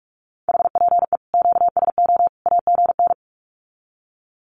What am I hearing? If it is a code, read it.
Morse code, heard as a keyed tone, about 44 wpm, 700 Hz. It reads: HPE QSO AGN
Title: Common QSO Phrases 45wpm